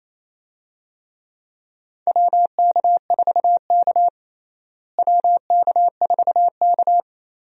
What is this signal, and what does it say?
Morse code, heard as a keyed tone, about 28 wpm, 700 Hz. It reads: WK4K WK4K
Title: USA Call Signs - 4 Characters - Repeated 3x - 28wpm